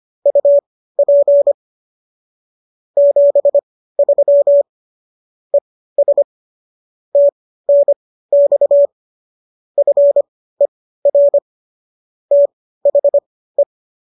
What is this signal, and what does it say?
Morse code, heard as a keyed tone, about 25 wpm, spaced as for 15 wpm, 570 Hz. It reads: UP 73 ES TNX FER THE